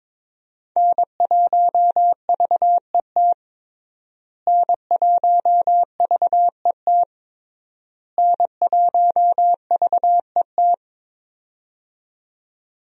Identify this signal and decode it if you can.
Morse code, heard as a keyed tone, about 22 wpm, 700 Hz. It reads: N1VET N1VET N1VET
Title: USA Call Signs - 5 Characters - Repeated 3x - 22wpm